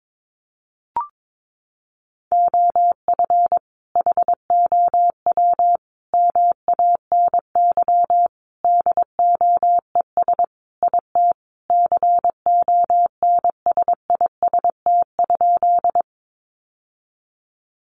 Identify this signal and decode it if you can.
Morse code, heard as a keyed tone, about 22 wpm, 700 Hz. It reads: OF HOW MANY DOES IT CONSIST?
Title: Sentences from Top 1400 Words 22wpm